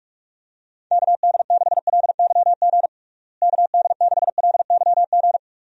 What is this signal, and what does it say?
Morse code, heard as a keyed tone, about 42 wpm, 700 Hz. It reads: KD6LYG KD6LYG
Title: USA Call Signs - 6 Characters - Repeated 2x - 45wpm